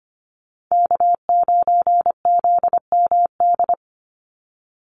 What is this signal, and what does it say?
Morse code, heard as a keyed tone, about 24 wpm, 700 Hz. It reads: K9ZMD
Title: NA SOTA Chaser Callsigns 25wpm